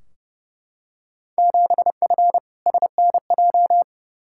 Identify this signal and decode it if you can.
Morse code, heard as a keyed tone, about 30 wpm, 700 Hz. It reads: 7F SNJ
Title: Field Day Exchanges Encore - 30wpm